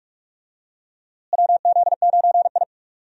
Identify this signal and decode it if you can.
Morse code, heard as a keyed tone, about 45 wpm, 700 Hz. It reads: WZ9I
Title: USA Call Signs - 4 Characters - Repeated 3x - 45wpm